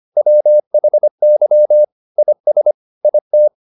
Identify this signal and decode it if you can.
Morse code, heard as a keyed tone, about 25 wpm, 600 Hz. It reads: WHY IS IT